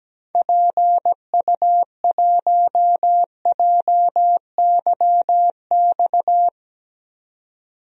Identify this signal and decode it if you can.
Morse code, heard as a keyed tone, about 17 wpm, 700 Hz. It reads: PU1JYX
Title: DXCC-100 Call Signs - 6 Characters - 17wpm